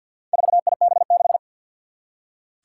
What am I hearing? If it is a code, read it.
Morse code, heard as a keyed tone, about 47 wpm, 700 Hz. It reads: 4IB6
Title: Four Challenging Letter-Number Combinations 50wpm